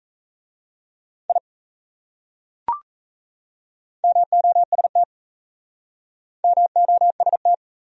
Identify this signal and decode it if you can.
Morse code, heard as a keyed tone, about 36 wpm, 700 Hz. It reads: I MOST MOST